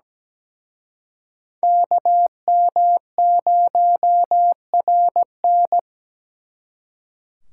Morse code, heard as a keyed tone, about 20 wpm, 700 Hz. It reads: KM0RN
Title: Speed Racing - USA 2x2 Call Signs 17wpm